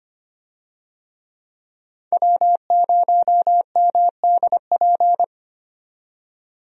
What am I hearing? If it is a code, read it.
Morse code, heard as a keyed tone, about 25 wpm, 700 Hz. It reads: W0MDP
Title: USA Call Signs - 5 Characters - 25wpm